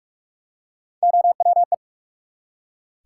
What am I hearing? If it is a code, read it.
Morse code, heard as a keyed tone, about 44 wpm, 700 Hz. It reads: OWE
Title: Three Letter Words - 45wpm